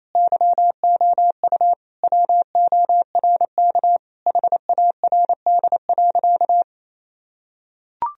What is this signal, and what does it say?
Morse code, heard as a keyed tone, about 28 wpm, 700 Hz. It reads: YOU WORK HARD.
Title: Sentences from Top 1000 Words - Review Part 1 - 28wpm